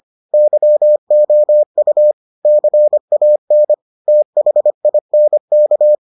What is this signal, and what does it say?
Morse code, heard as a keyed tone, about 25 wpm, 600 Hz. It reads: YOU CAN THINK